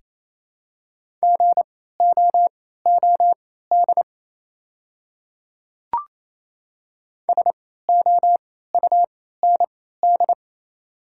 Morse code, heard as a keyed tone, about 27 wpm, spaced as for 16 wpm, 690 Hz. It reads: GOOD SOUND
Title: 3x Character Spacing - Top 200 Words - 28wpm